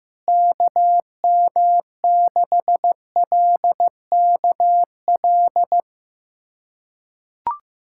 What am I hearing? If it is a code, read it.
Morse code, heard as a keyed tone, about 15 wpm, 700 Hz. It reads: KM6LKL
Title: USA Callsign DE Callsign - 6 Characters - 15wpm